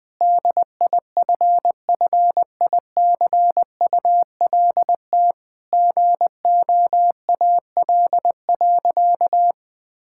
Morse code, heard as a keyed tone, about 20 wpm, 700 Hz. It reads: DIFFICULT GOAL.
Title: Sentences from Top 1000 Words - Review Part 5 - 20wpm